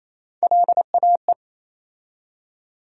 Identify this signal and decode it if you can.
Morse code, heard as a keyed tone, about 28 wpm, 700 Hz. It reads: LAE
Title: USA Call Sign Postfixes - 3 Letters 28wpm